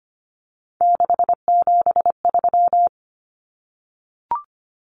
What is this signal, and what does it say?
Morse code, heard as a keyed tone, about 25 wpm, 700 Hz. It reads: 673
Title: Common Cut Numbers - 3-Digit 25wpm